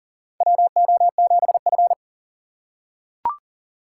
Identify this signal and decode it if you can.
Morse code, heard as a keyed tone, about 40 wpm, 700 Hz. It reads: WO7F
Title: USA Call Signs - 6x Character Spacing - 40wpm